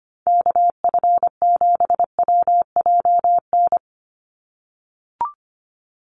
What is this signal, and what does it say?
Morse code, heard as a keyed tone, about 25 wpm, 700 Hz. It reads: KF7WJN
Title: USA Call Signs 25wpm